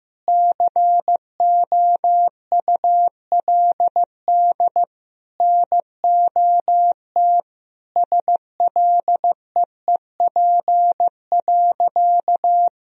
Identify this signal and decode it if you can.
Morse code, heard as a keyed tone, about 15 wpm, 700 Hz. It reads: COULD NOT SLEEP.
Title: Sentences from Top 1000 Words - Review Part 3 - 15wpm